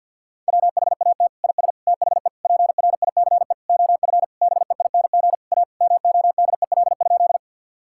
Morse code, heard as a keyed tone, about 50 wpm, 700 Hz. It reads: WHAT IS THE PRICE OF BEING A MODEL?
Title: Sentences from Top 1000 Words - Review Part 5 - 50wpm